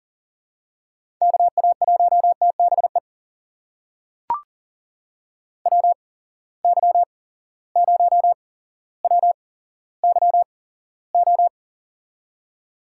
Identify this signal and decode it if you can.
Morse code, heard as a keyed tone, about 42 wpm, spaced as for 17 wpm, 700 Hz. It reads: KA1TBE WY0WYO
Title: USA Call Signs - 6 Characters - 8x Character Spacing - 40wpm